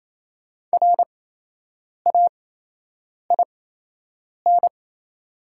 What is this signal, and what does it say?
Morse code, heard as a keyed tone, about 28 wpm, spaced as for 8 wpm, 700 Hz. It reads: RAIN
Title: 8x Character Spacing - Top 300 Words - 28wpm